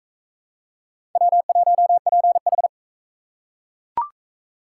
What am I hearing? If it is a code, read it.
Morse code, heard as a keyed tone, about 36 wpm, 700 Hz. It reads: W1PH